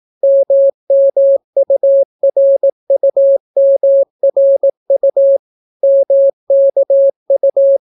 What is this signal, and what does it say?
Morse code, heard as a keyed tone, about 18 wpm, 550 Hz. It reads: MMURUMRU MKU